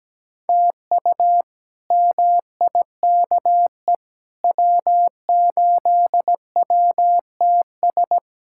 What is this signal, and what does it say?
Morse code, heard as a keyed tone, about 17 wpm, 700 Hz. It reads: TU MIKE W8WTS
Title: K1USN SST QSO Practice - 17wpm